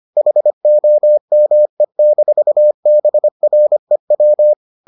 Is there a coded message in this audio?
Yes: Morse code from 0.2 to 4.5 s, heard as a keyed tone, about 25 wpm, 600 Hz. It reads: HOME-BREW